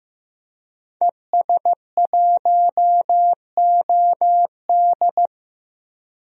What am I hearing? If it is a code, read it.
Morse code, heard as a keyed tone, about 15 wpm, 700 Hz. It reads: ES1OD
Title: DXCC-100 Call Signs - 5 Characters - Repeated 2x - 15wpm